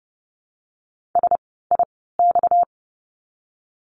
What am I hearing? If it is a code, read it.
Morse code, heard as a keyed tone, about 30 wpm, spaced as for 17 wpm, 700 Hz. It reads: SIX
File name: course / 30:10wpm / 230 Lesson 37f - Letter x - Words 30wpm